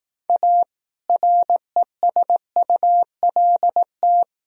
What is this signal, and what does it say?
Morse code, heard as a keyed tone, about 18 wpm, 700 Hz. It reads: A RESULT
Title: Speed-Racing - Sets of 3 Words 15wpm